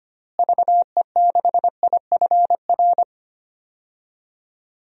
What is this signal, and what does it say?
Morse code, heard as a keyed tone, about 25 wpm, 700 Hz. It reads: VE6IFR
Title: DXCC-100 Call Signs - 6 Characters - Repeated 3x - 25wpm